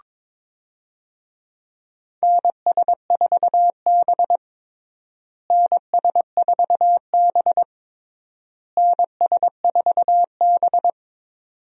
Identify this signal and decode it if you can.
Morse code, heard as a keyed tone, about 23 wpm, 700 Hz. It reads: NS4B NS4B NS4B
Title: USA Call Signs - 4 Characters - Repeated 3x - 22wpm